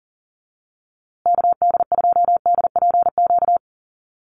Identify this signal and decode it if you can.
Morse code, heard as a keyed tone, about 37 wpm, 700 Hz. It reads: KD2DPQ
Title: USA Call Signs Repeated 2x 40wpm